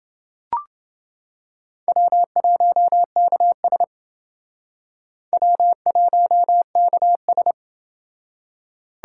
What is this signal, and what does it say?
Morse code, heard as a keyed tone, about 31 wpm, 700 Hz. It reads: W1KS W1KS